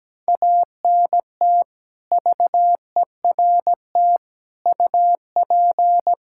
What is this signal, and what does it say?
Morse code, heard as a keyed tone, about 17 wpm, 700 Hz. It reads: ANT VERT UP
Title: Practice QSO - Exchange 2 - Info - 17wpm